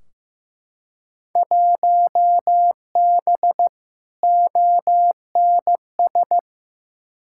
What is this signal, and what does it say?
Morse code, heard as a keyed tone, about 15 wpm, 700 Hz. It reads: 1B ONS
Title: Field Day Exchanges Encore - 15wpm